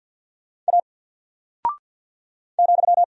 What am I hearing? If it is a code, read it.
Morse code, heard as a keyed tone, about 48 wpm, 700 Hz. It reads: A ,
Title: Single Character (Letter:Number:Punctuation:Prosign) 50wpm